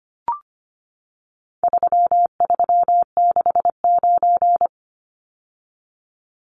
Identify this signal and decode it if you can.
Morse code, heard as a keyed tone, about 25 wpm, 700 Hz. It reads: 3369
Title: Numbers - 4-Digit 25wpm